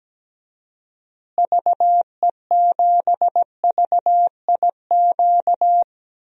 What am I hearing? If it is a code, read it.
Morse code, heard as a keyed tone, about 17 wpm, 700 Hz. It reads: VE7VIQ
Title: DXCC-100 Call Signs - 6 Characters - Repeated 2x - 17wpm